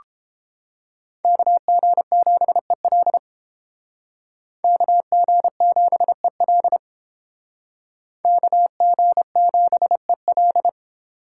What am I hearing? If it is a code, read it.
Morse code, heard as a keyed tone, about 31 wpm, 700 Hz. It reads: KG7EL KG7EL KG7EL